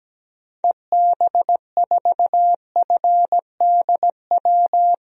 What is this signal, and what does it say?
Morse code, heard as a keyed tone, about 17 wpm, 700 Hz. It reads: EB4FDW
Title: DXCC-100 Call Signs - 6 Characters - 17wpm